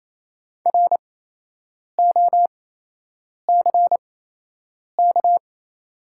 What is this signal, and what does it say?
Morse code, heard as a keyed tone, about 28 wpm, spaced as for 8 wpm, 700 Hz. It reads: ROCK
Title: 8x Character Spacing - Top 500 Words - 28wpm